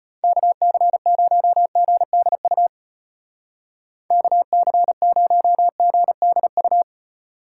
Morse code, heard as a keyed tone, about 32 wpm, 700 Hz. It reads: KC0GDU KC0GDU